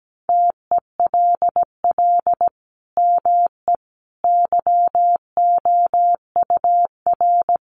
Morse code, heard as a keyed tone, about 17 wpm, 700 Hz. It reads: TELL ME YOUR
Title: Sentences from Top 1400 Words 17wpm